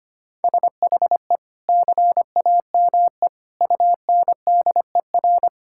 Morse code, heard as a keyed tone, about 25 wpm, 700 Hz. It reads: SHE CAME UNDER
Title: Sentences from Top 1000 Words - Review Part 6 - 25wpm